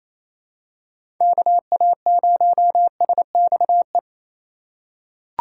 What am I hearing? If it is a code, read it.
Morse code, heard as a keyed tone, about 28 wpm, 700 Hz. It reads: KA0SXE
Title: USA Call Signs - 6 Characters - 4x Character Spacing - 28wpm